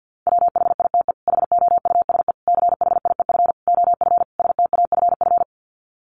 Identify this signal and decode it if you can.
Morse code, heard as a keyed tone, about 50 wpm, 700 Hz. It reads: WHITE HOUSE CHIEF OF STAFF
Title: Sets of 5 Words Encore 50wpm